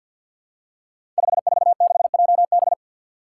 Morse code, heard as a keyed tone, about 50 wpm, 700 Hz. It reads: 546JB
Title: Five Challenging Letter-Number Combinations 50wpm